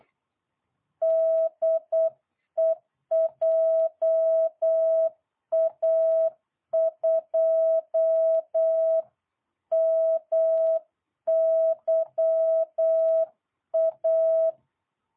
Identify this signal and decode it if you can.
Morse code, heard as a keyed tone, about 8 wpm, 660 Hz. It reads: DE JA2MYA